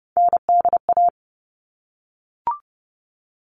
Morse code, heard as a keyed tone, about 30 wpm, 700 Hz. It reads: NDA
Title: All Cut Numbers - 3-Digit 30wpm